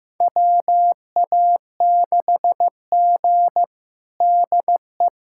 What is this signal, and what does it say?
Morse code, heard as a keyed tone, about 15 wpm, 700 Hz. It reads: WA6G DE
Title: USA Callsign DE Callsign - 4 Characters - 15wpm